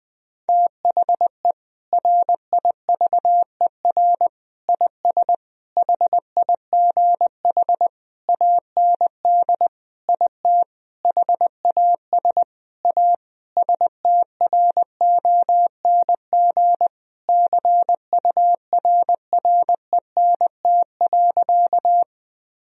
Morse code, heard as a keyed tone, about 20 wpm, 700 Hz. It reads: THE RIVER IS HIGH AND IT HAS A STRONG CURRENT.
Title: Sentences from Top 1000 Words - Review Part 4 - 20wpm